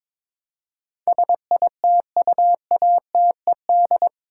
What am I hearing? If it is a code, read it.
Morse code, heard as a keyed tone, about 21 wpm, 700 Hz. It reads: SITUATED
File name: practice / 22wpm / Eight Letter Words - 22wpm